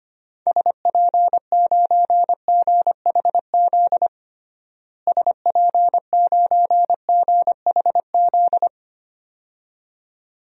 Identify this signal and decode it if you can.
Morse code, heard as a keyed tone, about 25 wpm, 700 Hz. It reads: SP9GHZ SP9GHZ
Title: DXCC-100 Call Signs - 6 Characters - Repeated 2x - 25wpm